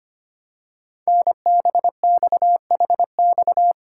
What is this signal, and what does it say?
Morse code, heard as a keyed tone, about 25 wpm, 700 Hz. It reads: NBXHX
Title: Code Groups - 5 Letters - 25wpm